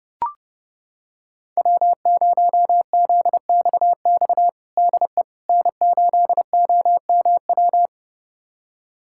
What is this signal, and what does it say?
Morse code, heard as a keyed tone, about 30 wpm, 700 Hz. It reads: W0ZXX DE N8OMW
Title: USA Callsign DE Callsign - 5 Characters - 30wpm